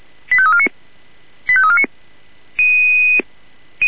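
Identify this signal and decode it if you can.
Five-tone selective call (ZVEI), heard as quick runs of five tones, tones of about 70 ms. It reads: 85349 85349
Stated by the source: ZVEI five-tone selective call 85349 received on SDR.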